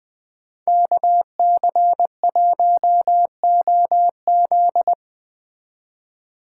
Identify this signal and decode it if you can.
Morse code, heard as a keyed tone, about 20 wpm, 700 Hz. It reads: KC1OZ
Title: Speed Racing - USA 2x2 Call Signs 17wpm